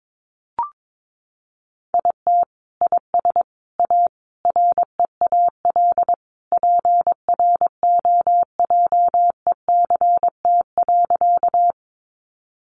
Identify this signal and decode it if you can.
Morse code, heard as a keyed tone, about 22 wpm, 700 Hz. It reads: IT IS A REAL PROJECT.
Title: Sentences from Top 1400 Words 22wpm